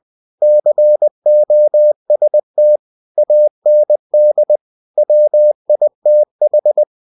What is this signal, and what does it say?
Morse code, heard as a keyed tone, about 20 wpm, 600 Hz. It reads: COST AND WITH
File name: cw-books / winnie-the-pooh / winnie-20wpm-page-0115